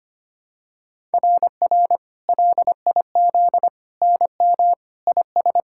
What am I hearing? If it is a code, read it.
Morse code, heard as a keyed tone, about 25 wpm, 700 Hz. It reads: RR LIZ NM IS
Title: SOTA QSO Practice 25wpm